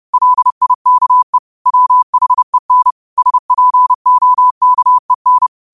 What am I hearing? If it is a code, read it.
Morse code, heard as a keyed tone, about 30 wpm, 1000 Hz. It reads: LIKE WHEN SPOKEN